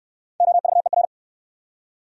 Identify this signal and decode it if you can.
Morse code, heard as a keyed tone, about 67 wpm, 700 Hz. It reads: OVER